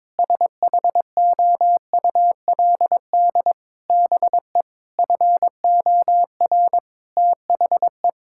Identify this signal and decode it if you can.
Morse code, heard as a keyed tone, about 22 wpm, 700 Hz. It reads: SHOULD BE FOR THE
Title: Sentences from Top 1000 Words - Review Part 6 - 22wpm